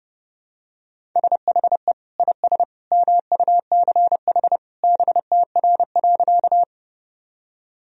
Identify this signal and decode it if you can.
Morse code, heard as a keyed tone, about 30 wpm, 700 Hz. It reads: SHE IS MUCH BTR.
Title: Sentences from Top 400 Words and Abbreviations 30wpm